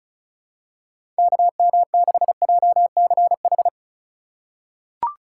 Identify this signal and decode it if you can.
Morse code, heard as a keyed tone, about 34 wpm, 700 Hz. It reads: KM6JCH
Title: USA Call Signs - 6 Characters - 2x Character Spacing - 35wpm